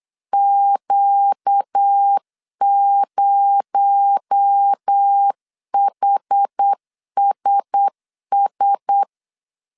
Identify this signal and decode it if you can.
Morse code, heard as a keyed tone, about 8 wpm, 800 Hz. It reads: Q0HSS
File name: CW / DET / 1 6 1500C S B 8wpm 5142